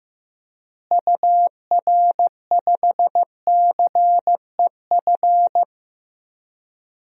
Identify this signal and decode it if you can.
Morse code, heard as a keyed tone, about 15 wpm, 700 Hz. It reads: UR5CEF
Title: DXCC-100 Call Signs - 6 Characters - Repeated 3x - 15wpm